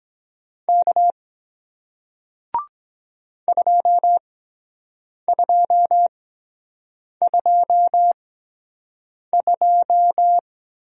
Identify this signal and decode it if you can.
Morse code, heard as a keyed tone, about 23 wpm, spaced as for 7 wpm, 700 Hz. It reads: K 2222